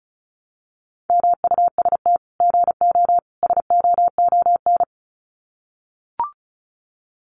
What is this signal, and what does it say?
Morse code, heard as a keyed tone, about 34 wpm, 700 Hz. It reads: MUST GO SOON
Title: Common QSO Phrases 35wpm